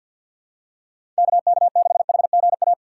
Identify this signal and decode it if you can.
Morse code, heard as a keyed tone, about 53 wpm, 700 Hz. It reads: KK6HGA
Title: USA Call Signs - 3x Character Spacing - 50wpm